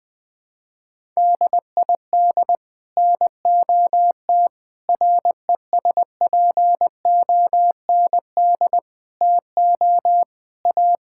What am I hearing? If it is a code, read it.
Morse code, heard as a keyed tone, about 20 wpm, 700 Hz. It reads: DID NOT RESPOND TO A
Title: Sets of 8 Words - 20wpm